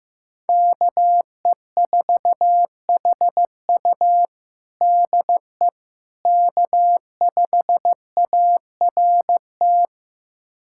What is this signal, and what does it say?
Morse code, heard as a keyed tone, about 15 wpm, 700 Hz. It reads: KE4HU DE K5ART
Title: USA Callsign DE Callsign - 5 Characters - 15wpm